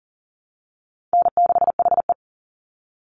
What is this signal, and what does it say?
Morse code, heard as a keyed tone, about 40 wpm, 700 Hz. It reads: N6HE
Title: Easy Call Signs - 40wpm